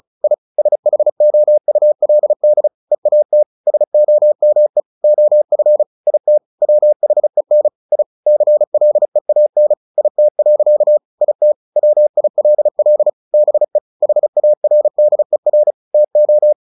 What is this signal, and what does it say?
Morse code, heard as a keyed tone, about 34 wpm, 600 Hz. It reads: I SHOULD EAT SOME OF IT WHEN I CLEAN IT. IT WILL BE HARDER TO
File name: the old man and the sea 2 35wpm